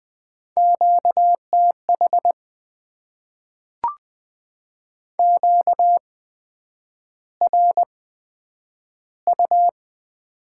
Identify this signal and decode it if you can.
Morse code, heard as a keyed tone, about 20 wpm, spaced as for 9 wpm, 700 Hz. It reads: QTH QRU
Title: Top 90 QSO Elements - 8x Character Spacing - 20wpm